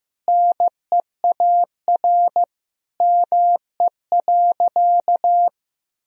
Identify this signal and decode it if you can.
Morse code, heard as a keyed tone, about 15 wpm, 700 Hz. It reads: NEAR ME.
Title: Sentences from Top 1000 Words - Review Part 1 - 15wpm